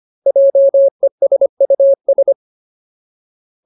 Morse code, heard as a keyed tone, about 25 wpm, 550 Hz. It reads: JESUS
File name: Ultimate Word List / 25WPM Ultimate